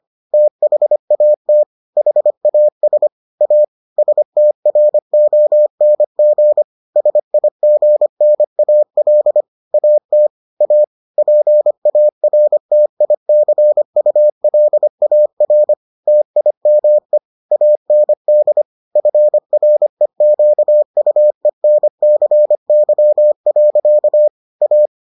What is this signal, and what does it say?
Morse code, heard as a keyed tone, about 25 wpm, 600 Hz. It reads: THAT HAS A STRONG SIGNAL AT A PARTICULAR TIME AND FREQUENCY. A